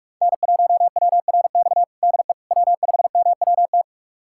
Morse code, heard as a keyed tone, about 46 wpm, 700 Hz. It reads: N1WRX DE W5MWT
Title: USA Callsign DE Callsign - 5 Characters - 45wpm